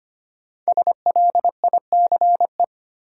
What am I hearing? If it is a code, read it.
Morse code, heard as a keyed tone, about 25 wpm, 700 Hz. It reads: SLICE
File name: 5 Letter Words - First 3 Letters Given - 25wpm